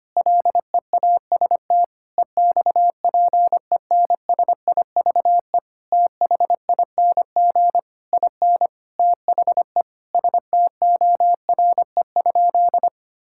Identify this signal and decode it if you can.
Morse code, heard as a keyed tone, about 25 wpm, 700 Hz. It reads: LEAST EXPENSIVE THING IN THE STORE?
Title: Sentences from Top 1000 Words - Review Part 6 - 25wpm